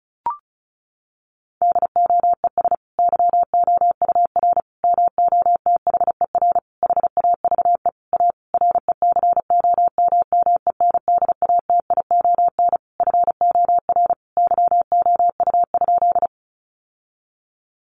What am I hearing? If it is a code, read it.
Morse code, heard as a keyed tone, about 34 wpm, 700 Hz. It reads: DOES YOUR MOTHER HAVE A RECOMMENDATION FOR YOU?
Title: Sentences from Top 1400 Words 35wpm